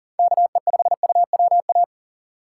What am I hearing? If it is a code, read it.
Morse code, heard as a keyed tone, about 40 wpm, 700 Hz. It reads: KE5UWA
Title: USA Call Signs - 6 Characters - 4x Character Spacing - 40wpm